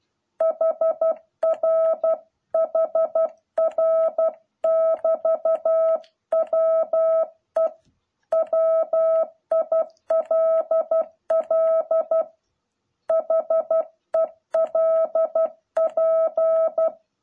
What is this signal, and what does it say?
Morse code, heard as a keyed tone, about 12 wpm, 660 Hz. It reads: HR HR <BT> WE WILL HELP